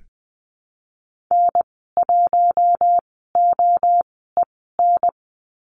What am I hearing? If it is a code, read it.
Morse code, heard as a keyed tone, about 20 wpm, spaced as for 14 wpm, 700 Hz. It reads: N1OEN
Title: 042 Lesson 8g - Number 1 - Call Signs 20wpm